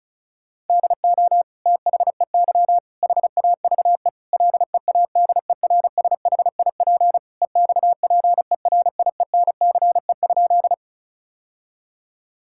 Morse code, heard as a keyed tone, about 34 wpm, 700 Hz. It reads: DO THEY HAVE LEADERSHIP EXPERIENCE?
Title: Sentences from Top 1600 Words 35wpm